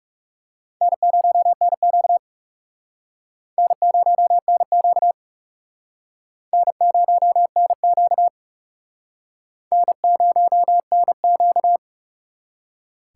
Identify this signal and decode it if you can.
Morse code, heard as a keyed tone, about 40 wpm, 700 Hz. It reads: N0NQ N0NQ N0NQ N0NQ